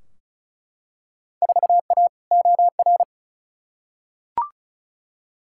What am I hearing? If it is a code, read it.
Morse code, heard as a keyed tone, about 36 wpm, 700 Hz. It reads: 4A OR
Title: Field Day Exchanges - 35wpm